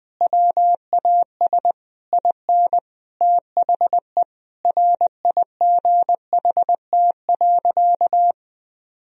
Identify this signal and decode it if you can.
Morse code, heard as a keyed tone, about 20 wpm, 700 Hz. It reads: WAS IN THE RIGHT.
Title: Sentences from Top 1000 Words - Review Part 1 - 20wpm